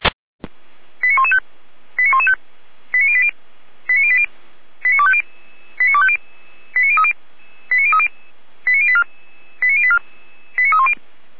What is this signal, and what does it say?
Five-tone selective call (ZVEI), heard as quick runs of five tones, tones of about 70 ms. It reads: 89115 89115 89080 89080 89250 89250 89030 89030 89084 89084 89310